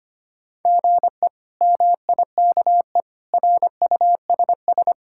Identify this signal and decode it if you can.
Morse code, heard as a keyed tone, about 25 wpm, 700 Hz. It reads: GE MIKE RUSS
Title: K1USN SST QSO Practice - 25wpm